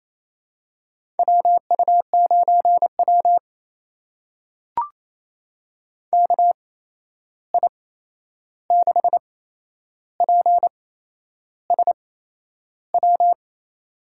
Morse code, heard as a keyed tone, about 28 wpm, spaced as for 11 wpm, 700 Hz. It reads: WU9W KI6PSW
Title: USA Call Signs - 8x Character Spacing - 28wpm